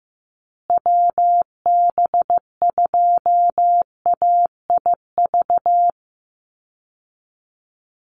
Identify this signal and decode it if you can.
Morse code, heard as a keyed tone, about 15 wpm, 700 Hz. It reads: WB2AIV
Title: Call Signs Repeated 3x - 15wpm